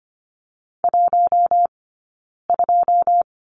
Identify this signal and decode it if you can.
Morse code, heard as a keyed tone, about 25 wpm, 700 Hz. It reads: 1 2
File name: course / 25:10wpm / 092 Lesson 16d - Letter c - Two Characters 25wpm